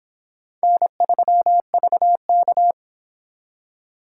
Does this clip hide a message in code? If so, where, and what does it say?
Yes: Morse code from 0.6 to 2.7 s, heard as a keyed tone, about 24 wpm, 700 Hz. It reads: N3VK